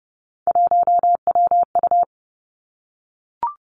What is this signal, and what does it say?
Morse code, heard as a keyed tone, about 30 wpm, 700 Hz. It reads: 1WU
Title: Three Challenging Letter-Number Combinations 30wpm